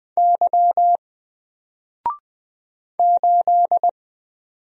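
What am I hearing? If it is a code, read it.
Morse code, heard as a keyed tone, about 20 wpm, 700 Hz. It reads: Y 8
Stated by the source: Single letter and number Morse drill at 20 wpm, from N4LCD.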